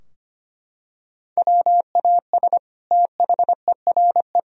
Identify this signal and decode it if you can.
Morse code, heard as a keyed tone, about 25 wpm, 700 Hz. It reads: WAS THERE
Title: Sentences from Top 1000 Words - Review Part 3 - 25wpm